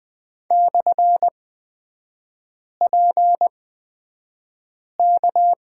Morse code, heard as a keyed tone, about 20 wpm, spaced as for 5 wpm, 700 Hz. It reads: /PK
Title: 236 Lesson 38e - Letter q - Three Characters 20wpm